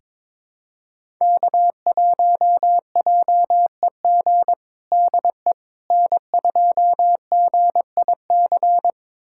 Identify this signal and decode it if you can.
Morse code, heard as a keyed tone, about 22 wpm, 700 Hz. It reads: K1JEG DE N2GIC
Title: USA Callsign DE Callsign 22wpm